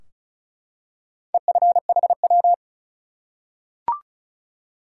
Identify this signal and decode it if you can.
Morse code, heard as a keyed tone, about 36 wpm, 700 Hz. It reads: EFHW
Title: Top 180 QSO Elements 35wpm